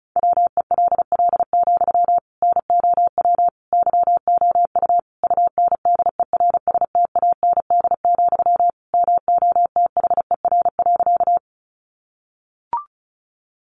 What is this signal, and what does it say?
Morse code, heard as a keyed tone, about 34 wpm, 700 Hz. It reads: WELL, NOW YOU UNDERSTAND, MOTHER.
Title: Sentences from Top 600 Words 35wpm